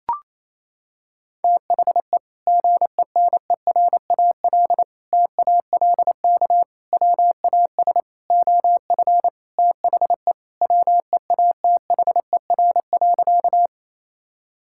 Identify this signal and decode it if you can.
Morse code, heard as a keyed tone, about 28 wpm, 700 Hz. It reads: THE GENERAL TALK WAS OF THE WEATHER.
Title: Sentences from Top 1000 Words - Review Part 5 - 28wpm